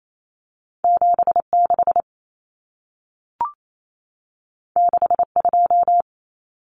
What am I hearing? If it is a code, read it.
Morse code, heard as a keyed tone, about 28 wpm, 700 Hz. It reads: 76 62
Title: Common Cut Numbers - 2-Digit 28wpm